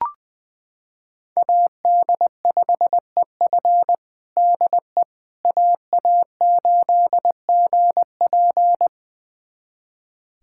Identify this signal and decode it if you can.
Morse code, heard as a keyed tone, about 20 wpm, 700 Hz. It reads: AD5EF DE AA8GP
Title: USA Callsign DE Callsign - 5 Characters - 20wpm